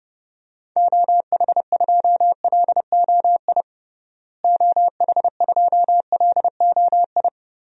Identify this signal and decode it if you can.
Morse code, heard as a keyed tone, about 30 wpm, 700 Hz. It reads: OH2LOI OH2LOI
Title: DXCC-100 Call Signs - 6 Characters - Repeated 3x - 30wpm